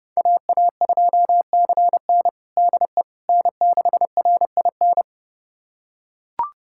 Morse code, heard as a keyed tone, about 32 wpm, 700 Hz. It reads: AA2CN DE N6RIN
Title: USA Callsign DE Callsign - 5 Characters - 30wpm